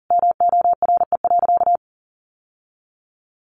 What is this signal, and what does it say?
Morse code, heard as a keyed tone, about 40 wpm, 700 Hz. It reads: MORE.
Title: Sentences from Top 1100 Words 40wpm